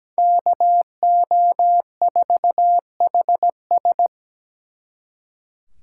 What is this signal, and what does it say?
Morse code, heard as a keyed tone, about 20 wpm, 700 Hz. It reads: KO4HS
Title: Speed Racing - USA Call Signs 17wpm